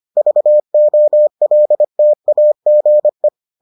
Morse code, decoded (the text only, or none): VOLTAGE